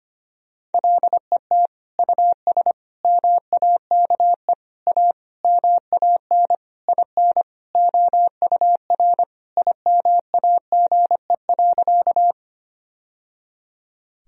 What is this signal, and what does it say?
Morse code, heard as a keyed tone, about 25 wpm, 700 Hz. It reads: LET US MAKE A MAN IN OUR IMAGE.
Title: Sentences from Top 1000 Words - Review Part 4 - 25wpm